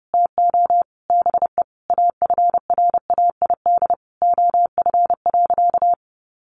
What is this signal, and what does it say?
Morse code, heard as a keyed tone, about 30 wpm, 700 Hz. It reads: TO BE AFRAID OF.
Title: Sentences from Top 1500 Words 30wpm